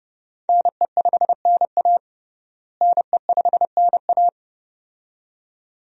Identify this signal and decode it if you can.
Morse code, heard as a keyed tone, about 30 wpm, 700 Hz. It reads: NE5NA NE5NA
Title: USA Call Signs - 5 Characters - Repeated 2x - 30wpm